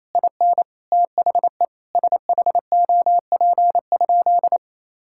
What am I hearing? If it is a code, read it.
Morse code, heard as a keyed tone, about 28 wpm, 700 Hz. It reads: IN THE SHOP?
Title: Sentences from Top 1600 Words 28wpm